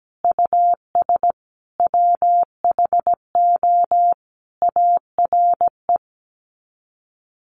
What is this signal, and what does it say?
Morse code, heard as a keyed tone, about 17 wpm, 700 Hz. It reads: US WHO ARE
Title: Sets of 5 Words Encore2 17wpm